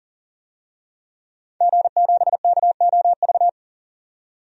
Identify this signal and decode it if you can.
Morse code, heard as a keyed tone, about 40 wpm, 700 Hz. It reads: G7KOV
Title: DXCC-100 Call Signs - 5 Characters - 40wpm